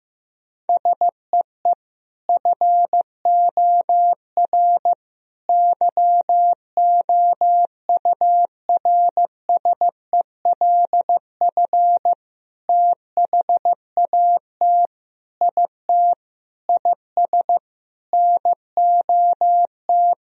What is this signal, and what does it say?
Morse code, heard as a keyed tone, about 15 wpm, 700 Hz. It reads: SEE FOR YOURSELF THAT IT IS NOT
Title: Sentences from Top 1000 Words - Review Part 1 - 15wpm